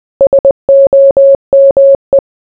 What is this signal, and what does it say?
Morse code, heard as a keyed tone, about 20 wpm, 550 Hz. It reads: SOME